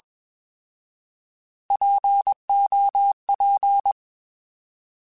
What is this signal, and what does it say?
Morse code, heard as a keyed tone, about 21 wpm, 800 Hz. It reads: POP
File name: Alphabetical / Alphabetical mp3s / Ps @20